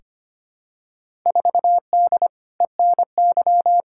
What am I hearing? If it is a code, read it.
Morse code, heard as a keyed tone, about 25 wpm, 700 Hz. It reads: 4D ENY
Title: Field Day Exchanges Encore - 25wpm